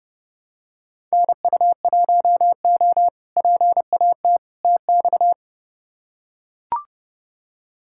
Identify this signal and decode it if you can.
Morse code, heard as a keyed tone, about 30 wpm, 700 Hz. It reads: NU1O PAT TX
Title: K1USN SST QSO Practice - 30wpm